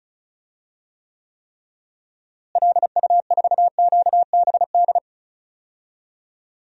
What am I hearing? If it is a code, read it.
Morse code, heard as a keyed tone, about 36 wpm, 700 Hz. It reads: LU4QBD
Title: DXCC-100 Call Signs - 6 Characters - Repeated 3x - 35wpm